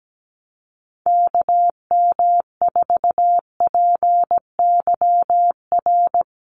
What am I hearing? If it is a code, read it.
Morse code, heard as a keyed tone, about 17 wpm, 700 Hz. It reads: KM4PYR
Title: USA Call Signs Repeated 2x 17wpm